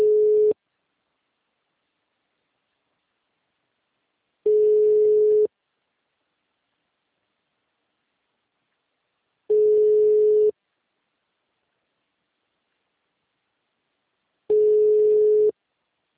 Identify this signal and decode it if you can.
Ringback tone (Europe, 425 Hz).